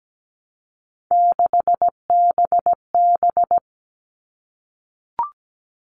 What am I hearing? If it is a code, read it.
Morse code, heard as a keyed tone, about 17 wpm, 700 Hz. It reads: 6BB
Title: All Cut Numbers - 3-Digit 17wpm